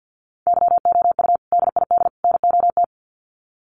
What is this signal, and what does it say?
Morse code, heard as a keyed tone, about 49 wpm, 700 Hz. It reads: YOU DID NOT
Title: Sets of 3 Words 50wpm